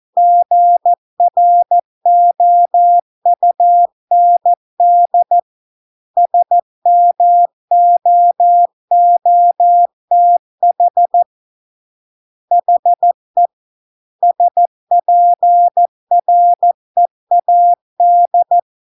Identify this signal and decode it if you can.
Morse code, heard as a keyed tone, about 14 wpm, 700 Hz. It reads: GROUND SMOOTH HE SPREAD